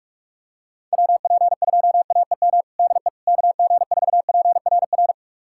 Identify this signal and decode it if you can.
Morse code, heard as a keyed tone, about 41 wpm, 700 Hz. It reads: WP2AEM DE KG4PRR
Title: USA Callsign DE Callsign - 6 Characters - 45wpm